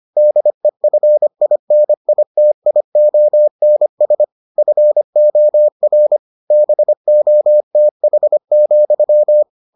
Morse code, heard as a keyed tone, about 25 wpm, 600 Hz. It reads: DEFINITIONS FOR BOTH,